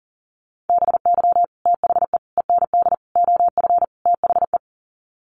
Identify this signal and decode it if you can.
Morse code, heard as a keyed tone, about 38 wpm, 700 Hz. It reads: BY THE END OF THE
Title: Sets of 5 Words 40wpm